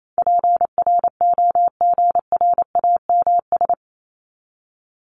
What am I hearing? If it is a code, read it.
Morse code, heard as a keyed tone, about 28 wpm, 700 Hz. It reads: PROGRAMS
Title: ES:S Suffix - Top 100 Words with Base Words 28wpm